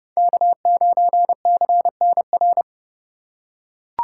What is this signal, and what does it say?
Morse code, heard as a keyed tone, about 32 wpm, 700 Hz. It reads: K9CNR
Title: USA Call Signs - 5 Characters - 2x Character Spacing - 30wpm